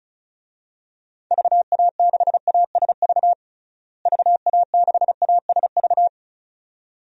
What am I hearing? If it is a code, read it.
Morse code, heard as a keyed tone, about 36 wpm, 700 Hz. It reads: VA6ASV VA6ASV
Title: DXCC-100 Call Signs - 6 Characters - Repeated 2x - 35wpm